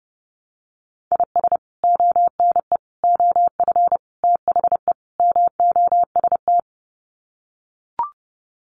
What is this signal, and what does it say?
Morse code, heard as a keyed tone, about 30 wpm, 700 Hz. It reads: IS ONE OF THE MOST
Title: Sets of 5 Words 30wpm